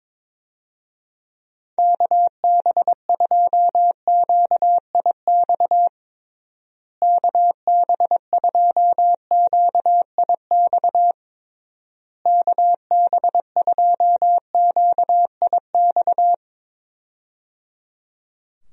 Morse code, heard as a keyed tone, about 22 wpm, 700 Hz. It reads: KB2QIX KB2QIX KB2QIX
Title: USA Call Signs - 6 Characters - Repeated 3x - 22wpm